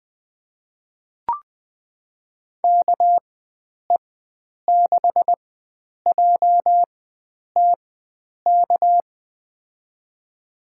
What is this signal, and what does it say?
Morse code, heard as a keyed tone, about 20 wpm, spaced as for 9 wpm, 700 Hz. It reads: KE6JTK
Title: USA Call Signs - 6 Characters - 4x Character Spacing - 20wpm